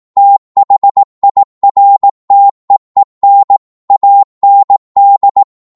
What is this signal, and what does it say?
Morse code, heard as a keyed tone, about 18 wpm, 800 Hz. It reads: THIRTEEN AND